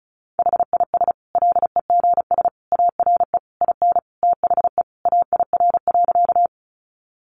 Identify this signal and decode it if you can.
Morse code, heard as a keyed tone, about 34 wpm, 700 Hz. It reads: HIS LEGS ARE IN THE AIR.
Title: Additional Easy Phrases 35wpm